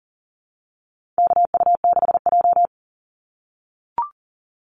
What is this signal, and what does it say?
Morse code, heard as a keyed tone, about 40 wpm, 700 Hz. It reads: KU6J
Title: NA SOTA Activator Callsigns 40wpm